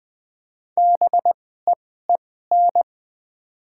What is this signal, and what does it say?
Morse code, heard as a keyed tone, about 20 wpm, spaced as for 15 wpm, 700 Hz. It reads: BEEN
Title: 2x Character Spacing - Top 300 Words - 20wpm